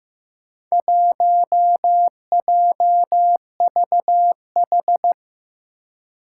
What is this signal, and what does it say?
Morse code, heard as a keyed tone, about 15 wpm, 700 Hz. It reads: 1JVH
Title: Four Challenging Letter-Number Combinations 15wpm